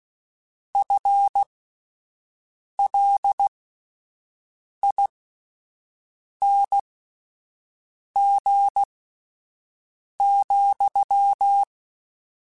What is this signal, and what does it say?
Morse code, heard as a keyed tone, about 16 wpm, spaced as for 5 wpm, 800 Hz. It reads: FLING,